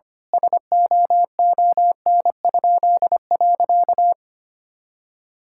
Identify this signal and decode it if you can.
Morse code, heard as a keyed tone, about 25 wpm, 700 Hz. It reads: SOON?.
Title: Pangram Challenge v2 - 25wpm